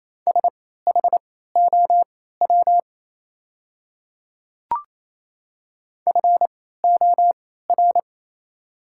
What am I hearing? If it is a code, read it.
Morse code, heard as a keyed tone, about 27 wpm, spaced as for 16 wpm, 700 Hz. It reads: SHOW FOR
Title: 3x Character Spacing - Top 200 Words - 28wpm